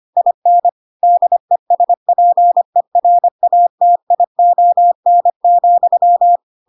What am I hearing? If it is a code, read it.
Morse code, heard as a keyed tone, about 25 wpm, 700 Hz. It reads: IN DESPERATION,